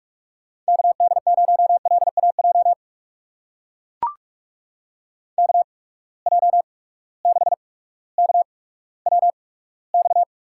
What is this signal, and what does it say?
Morse code, heard as a keyed tone, about 44 wpm, spaced as for 19 wpm, 700 Hz. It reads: KD0LAJ KJ6KWX
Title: USA Call Signs - 6 Characters - 8x Character Spacing - 45wpm